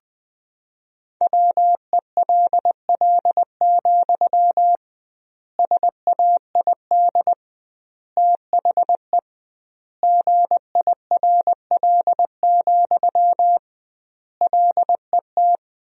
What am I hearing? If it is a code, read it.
Morse code, heard as a keyed tone, about 20 wpm, 700 Hz. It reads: WELL, SAID THE GIRL, LET
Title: 2x Word-Spacing - Sentences from Top 400 Words - 20wpm